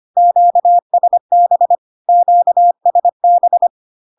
Morse code, heard as a keyed tone, about 25 wpm, 700 Hz. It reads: QSB QSB
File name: Q-codes most common repeated 3 times 25 25wpm